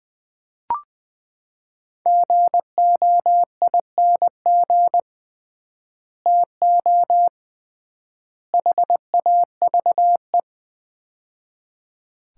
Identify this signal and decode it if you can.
Morse code, heard as a keyed tone, about 20 wpm, 700 Hz. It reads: GOING TO HAVE
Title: 3x Word-Spacing - Sets of 3 Words - 20wpm